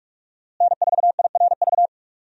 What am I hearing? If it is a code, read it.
Morse code, heard as a keyed tone, about 42 wpm, 700 Hz. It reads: N4IRV
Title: USA Call Signs - 5 Characters - Repeated 2x - 45wpm